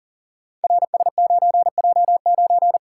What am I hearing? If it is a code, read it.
Morse code, heard as a keyed tone, about 43 wpm, 690 Hz. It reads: RS9J9
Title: Code Groups - 5 Letter:Numbers - 40wpm